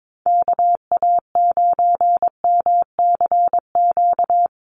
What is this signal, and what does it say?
Morse code, heard as a keyed tone, about 22 wpm, 700 Hz. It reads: KA9MCQ
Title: USA Call Signs 22wpm